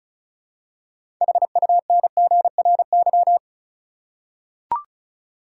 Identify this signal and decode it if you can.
Morse code, heard as a keyed tone, about 36 wpm, 700 Hz. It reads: HUNGRY
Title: Six Letter Words - 35wpm